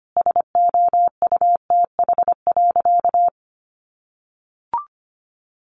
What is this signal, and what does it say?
Morse code, heard as a keyed tone, about 25 wpm, 700 Hz. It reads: SOUTH.
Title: Sentences from Top 400 Words 25wpm